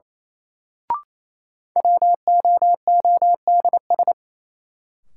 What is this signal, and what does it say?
Morse code, heard as a keyed tone, about 28 wpm, 700 Hz. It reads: WOODS
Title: Sentences from Top 1000 Words - Review Part 2 - New Word Drill - 28wpm